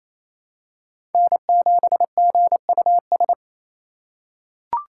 Morse code, heard as a keyed tone, about 28 wpm, 700 Hz. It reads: N7GUS
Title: USA Call Signs - 4x Character Spacing - 28wpm